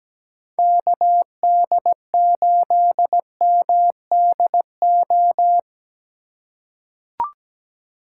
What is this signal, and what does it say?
Morse code, heard as a keyed tone, about 17 wpm, 700 Hz. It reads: KD8MDO
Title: USA Call Signs - 4x Character Spacing - 17wpm